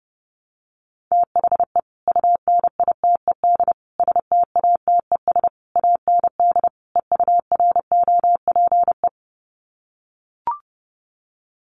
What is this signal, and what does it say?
Morse code, heard as a keyed tone, about 30 wpm, 700 Hz. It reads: THE UNITED STATES AND EUROPE
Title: Sets of 5 Words Encore 30wpm